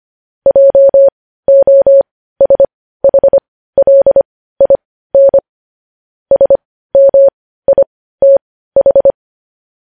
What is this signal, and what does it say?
Morse code, heard as a keyed tone, about 25 wpm, spaced as for 15 wpm, 550 Hz. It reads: JOSHLIN SMITH
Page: news-headlines-in-morse-code-at-15-wpm